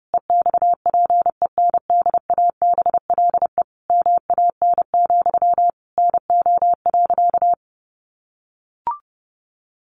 Morse code, heard as a keyed tone, about 30 wpm, 700 Hz. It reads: EXPENDABLE MAN, NO.
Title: A Sound Of Thunder 30wpm